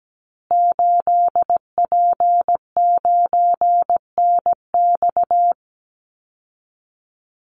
Morse code, heard as a keyed tone, about 17 wpm, 700 Hz. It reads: 8P9NX
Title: Call Signs Repeated 2x - 17wpm